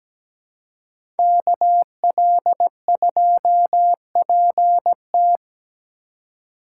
Morse code, heard as a keyed tone, about 17 wpm, 700 Hz. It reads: KL2PT
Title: USA Call Signs - 5 Characters - 3x Character Spacing - 17wpm